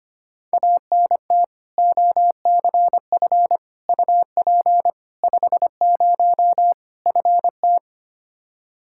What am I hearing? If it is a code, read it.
Morse code, heard as a keyed tone, about 25 wpm, 700 Hz. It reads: ANT OCF UP 50 FT
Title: Practice QSO - Exchange 2 - Info - 25wpm